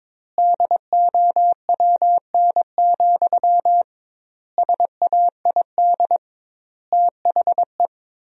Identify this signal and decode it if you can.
Morse code, heard as a keyed tone, about 22 wpm, 700 Hz. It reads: DOWN, SAID THE
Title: 2x Word-Spacing - Sentences from Top 500 Words - 22wpm